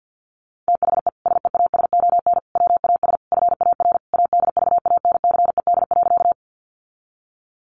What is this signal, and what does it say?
Morse code, heard as a keyed tone, about 50 wpm, 700 Hz. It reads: THE SEASON WAS FAR ADVANCED.
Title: Sentences from Top 1400 Words 50wpm